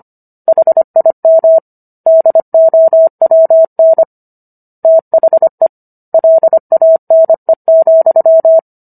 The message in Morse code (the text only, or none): HIM DOWN THE LANE,